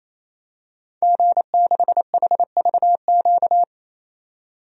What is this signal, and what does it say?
Morse code, heard as a keyed tone, about 27 wpm, 700 Hz. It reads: G6HVQ
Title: DXCC-100 Call Signs - 5 Characters - 28wpm